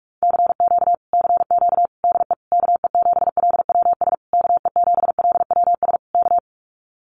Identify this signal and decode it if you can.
Morse code, heard as a keyed tone, about 44 wpm, 700 Hz. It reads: CQ CQ DE KE7LWS KE7LWS K
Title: Common QSO Phrases Encore 45wpm